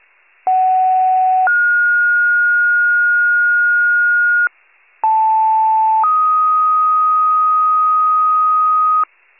Two-tone page, heard as a one-second tone, then a three-second tone, about 750 then 1430 Hz, about 880 then 1250 Hz, with noise. It is heard from 0.5 to 9.0 s.
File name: wcalert